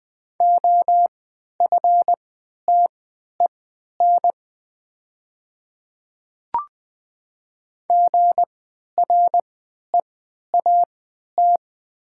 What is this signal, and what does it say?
Morse code, heard as a keyed tone, about 20 wpm, spaced as for 11 wpm, 700 Hz. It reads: OFTEN GREAT
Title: 3x Character Spacing - Top 300 Words - 20wpm